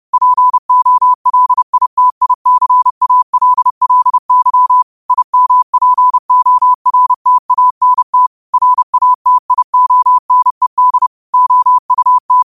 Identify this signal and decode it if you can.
Morse code, heard as a keyed tone, about 30 wpm, 1000 Hz. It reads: POLITICALLY IMPORTANT RATIONED OUT